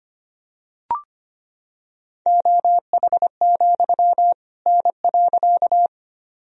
Morse code, heard as a keyed tone, about 25 wpm, 700 Hz. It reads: OH, N.
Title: Sentences from Top 500 Words and Abbreviations 25wpm